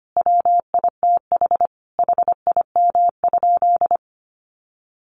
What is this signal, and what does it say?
Morse code, heard as a keyed tone, about 25 wpm, 700 Hz. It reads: WITH HIM?
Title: Sentences from Top 700 Words 25wpm